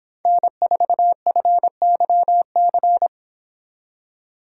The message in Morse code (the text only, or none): N4FYC